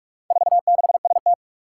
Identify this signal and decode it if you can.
Morse code, heard as a keyed tone, about 44 wpm, 700 Hz. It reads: 46ST
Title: Four Challenging Letter-Number Combinations 45wpm